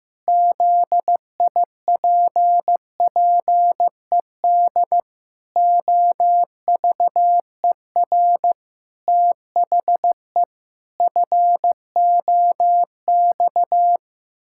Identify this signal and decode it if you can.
Morse code, heard as a keyed tone, about 15 wpm, 700 Hz. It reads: ZIPPED OVER THE FOX
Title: Pangram Challenge v2 - 15wpm